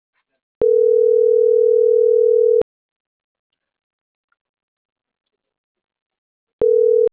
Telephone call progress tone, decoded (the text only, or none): ringback tone